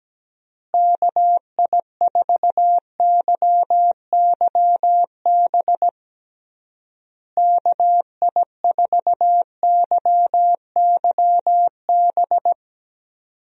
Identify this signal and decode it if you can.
Morse code, heard as a keyed tone, about 17 wpm, 700 Hz. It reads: KI4YYB KI4YYB
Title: USA Call Signs - 6 Characters - Repeated 3x - 17wpm